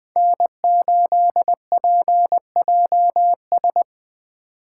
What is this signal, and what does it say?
Morse code, heard as a keyed tone, about 20 wpm, 700 Hz. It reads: N8PJS
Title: Speed Racing - USA 1x3 Call Signs 17wpm